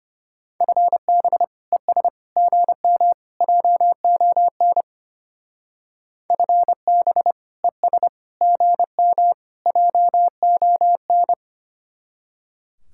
Morse code, heard as a keyed tone, about 31 wpm, 700 Hz. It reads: FB ES GM JON FB ES GM JON
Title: Speed-Racing Common QSO Phrases Encore - 25wpm